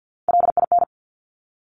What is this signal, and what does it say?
Morse code, heard as a keyed tone, about 50 wpm, 700 Hz. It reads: LIN
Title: Top Three Letter Combinations in Words 50wpm